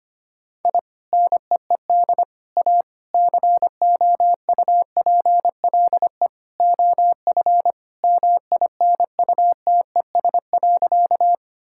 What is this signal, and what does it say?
Morse code, heard as a keyed tone, about 25 wpm, 700 Hz. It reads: I NEED A COUPLE OF MINUTES.
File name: practice / 25wpm / Sentences with 3-Word Phrases - 25wpm